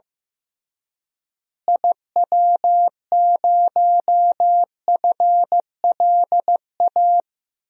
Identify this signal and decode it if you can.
Morse code, heard as a keyed tone, about 16 wpm, 700 Hz. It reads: IW0FLA
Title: DXCC-100 Call Signs - 6 Characters - Repeated 3x - 15wpm